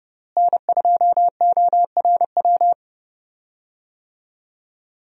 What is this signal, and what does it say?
Morse code, heard as a keyed tone, about 30 wpm, 700 Hz. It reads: N2ORW
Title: USA Call Signs - 5 Characters - Repeated 3x - 30wpm